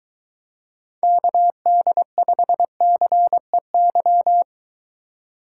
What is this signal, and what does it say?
Morse code, heard as a keyed tone, about 22 wpm, 700 Hz. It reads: KD5CEY